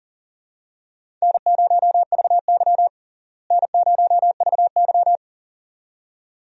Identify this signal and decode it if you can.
Morse code, heard as a keyed tone, about 40 wpm, 700 Hz. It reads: N0VY N0VY
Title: USA Call Signs - 4 Characters - Repeated 2x - 40wpm